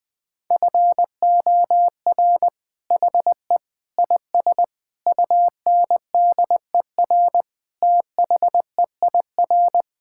Morse code, heard as a keyed tone, about 20 wpm, 700 Hz. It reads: FOR HE IS UNDER THEIR
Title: Sentences from Top 1600 Words 20wpm